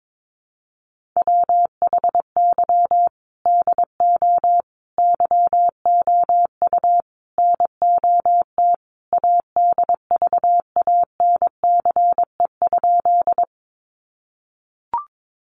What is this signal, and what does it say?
Morse code, heard as a keyed tone, about 22 wpm, 700 Hz. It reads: WHY DO YOU NOT ADVANCE?
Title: Sentences from Top 1400 Words 22wpm